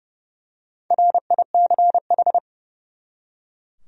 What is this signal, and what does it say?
Morse code, heard as a keyed tone, about 30 wpm, 700 Hz. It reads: RICH
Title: USA Common Names - Max 5 Letters - 30wpm